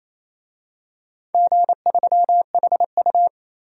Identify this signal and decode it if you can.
Morse code, heard as a keyed tone, about 28 wpm, 700 Hz. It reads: G3HU
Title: DXCC-100 Call Signs - 4 Characters - 28wpm